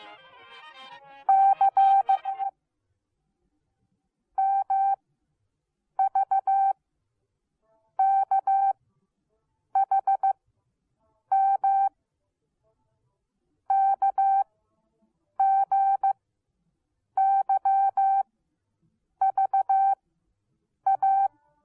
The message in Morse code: C MVKHM KGYVA

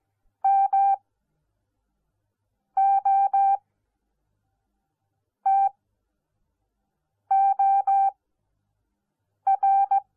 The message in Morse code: MOTOR

